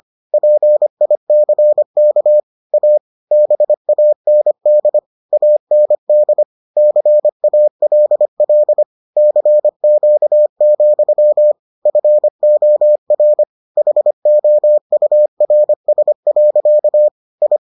Morse code, heard as a keyed tone, about 25 wpm, 600 Hz. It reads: PICK A BAND AND CALL CQ, FOR HOURS. I